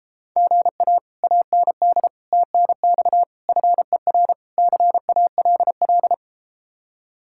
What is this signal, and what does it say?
Morse code, heard as a keyed tone, about 33 wpm, 700 Hz. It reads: GA AND TNX FER CALL